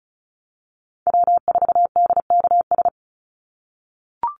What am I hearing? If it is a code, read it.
Morse code, heard as a keyed tone, about 34 wpm, 700 Hz. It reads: W4DKS
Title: Call Signs - 35wpm